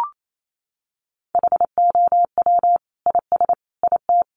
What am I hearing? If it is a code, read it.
Morse code, heard as a keyed tone, about 28 wpm, 700 Hz. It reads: HOW IS IT
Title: Sentences from Top 900 Words 28wpm